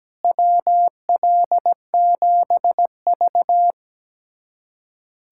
Morse code, heard as a keyed tone, about 17 wpm, 700 Hz. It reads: WL7V
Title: USA Callsign DE Callsign - 4 Characters - 17wpm